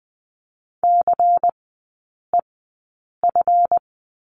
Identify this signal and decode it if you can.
Morse code, heard as a keyed tone, about 20 wpm, spaced as for 8 wpm, 700 Hz. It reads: CEF
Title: 152 Lesson 25e - Letter p - Three Characters 20wpm